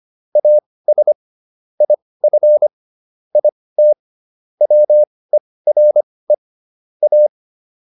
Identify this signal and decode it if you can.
Morse code, heard as a keyed tone, about 25 wpm, spaced as for 18 wpm, 600 Hz. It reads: AS IF IT WERE A